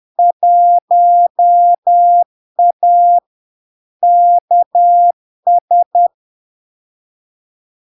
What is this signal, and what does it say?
Morse code, heard as a keyed tone, about 10 wpm, 700 Hz. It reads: 1A KS
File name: LARK-FD-2022-Log-in-CW-10wpm-700-Hz